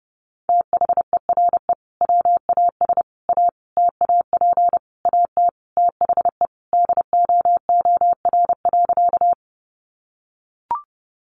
Morse code, heard as a keyed tone, about 28 wpm, 700 Hz. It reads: THERE WAS A TAP AT THE DOOR.
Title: Sentences from Top 1400 Words 30wpm